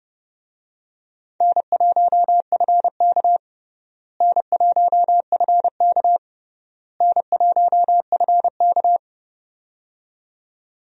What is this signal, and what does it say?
Morse code, heard as a keyed tone, about 30 wpm, 700 Hz. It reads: N1FK N1FK N1FK
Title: USA Call Signs - 4 Characters - Repeated 3x - 30wpm